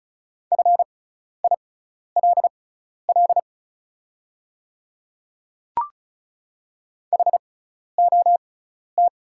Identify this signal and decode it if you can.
Morse code, heard as a keyed tone, about 36 wpm, spaced as for 12 wpm, 690 Hz. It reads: FILL HOT
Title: 6x Character Spacing - Top 500 Words - 35wpm